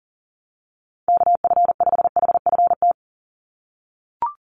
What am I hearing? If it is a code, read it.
Morse code, heard as a keyed tone, about 40 wpm, 700 Hz. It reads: KF5HFT
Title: USA Call Signs 40wpm